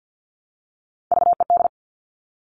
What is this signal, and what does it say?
Morse code, heard as a keyed tone, about 50 wpm, 700 Hz. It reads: VED
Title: All Cut Numbers - 3-Digit 50wpm